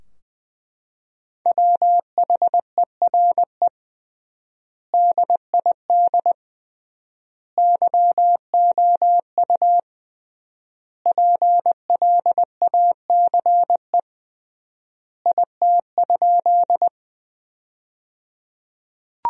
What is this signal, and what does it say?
Morse code, heard as a keyed tone, about 20 wpm, 700 Hz. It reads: WHERE DID YOU PLACE IT?
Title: 3x Word-Spacing - Sentences from Top 200 Words - 20wpm